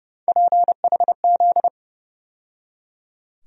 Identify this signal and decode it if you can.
Morse code, heard as a keyed tone, about 32 wpm, 700 Hz. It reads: PHZ
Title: USA Call Sign Postfixes - 3 Letters 30wpm